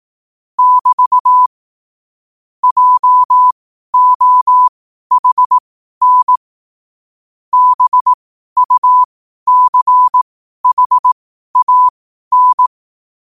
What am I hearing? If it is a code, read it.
Morse code, heard as a keyed tone, about 18 wpm, spaced as for 12 wpm, 1000 Hz. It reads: <BT> JOHN BUCHAN
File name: qotdfiles / 12wpm